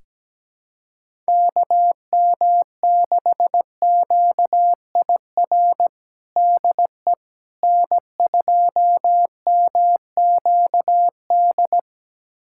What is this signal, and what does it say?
Morse code, heard as a keyed tone, about 17 wpm, 700 Hz. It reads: KM6QIR DE N2MQD
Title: USA Callsign DE Callsign 17wpm